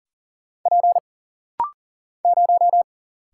Morse code, heard as a keyed tone, about 40 wpm, 700 Hz. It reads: P 0
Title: Single Letter-Number Alt - Rapid-Fire 40wpm